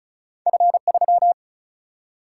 Morse code, heard as a keyed tone, about 36 wpm, 700 Hz. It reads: F3
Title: Single-Letter Single-Number 35wpm